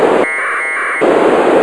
Packet radio (AX.25, as in APRS).